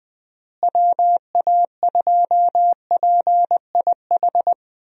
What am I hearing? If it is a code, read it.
Morse code, heard as a keyed tone, about 20 wpm, 700 Hz. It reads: WA2PIH
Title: USA Call Signs - 6 Characters - Repeated 3x - 20wpm